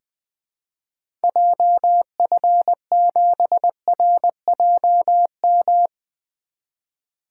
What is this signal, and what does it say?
Morse code, heard as a keyed tone, about 20 wpm, 700 Hz. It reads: JF7RJM
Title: 2023 POTA Activator Call Signs - 20wpm